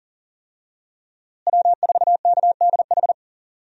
Morse code, heard as a keyed tone, about 40 wpm, 700 Hz. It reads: W4KDH
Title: USA Call Signs - 5 Characters - Repeated 2x - 40wpm